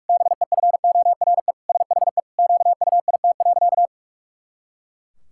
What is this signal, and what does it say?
Morse code, heard as a keyed tone, about 46 wpm, 700 Hz. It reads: BEFORE SHE QUIT.
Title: Pangram Challenge v2 - 45wpm